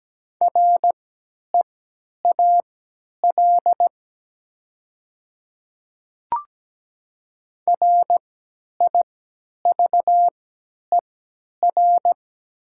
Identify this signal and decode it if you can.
Morse code, heard as a keyed tone, about 17 wpm, spaced as for 10 wpm, 700 Hz. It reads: REAL RIVER
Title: 3x Character Spacing - Top 500 Words - 17wpm